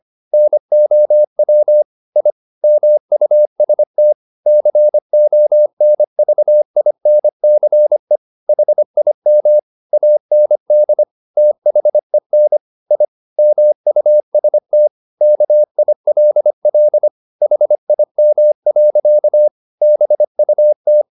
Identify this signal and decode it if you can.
Morse code, heard as a keyed tone, about 25 wpm, 600 Hz. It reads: NOW I MUST CONVINCE HIM AND THEN I MUST KILL HIM. BUT